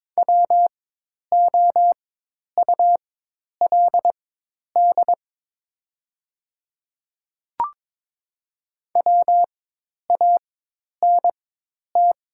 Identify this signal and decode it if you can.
Morse code, heard as a keyed tone, about 22 wpm, spaced as for 10 wpm, 700 Hz. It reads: WOULD WANT
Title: 4x Character Spacing - Top 200 Words - 22wpm